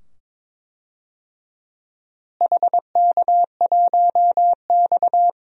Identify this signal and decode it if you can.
Morse code, heard as a keyed tone, about 22 wpm, 700 Hz. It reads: HK1X
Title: DXCC-100 Call Signs - 4 Characters - Repeated 2x - 22wpm